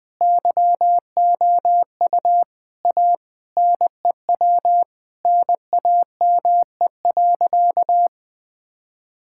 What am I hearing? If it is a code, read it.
Morse code, heard as a keyed tone, about 20 wpm, 700 Hz. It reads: YOU A NEW NAME.
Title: Sentences from Top 1000 Words - Review Part 5 - 20wpm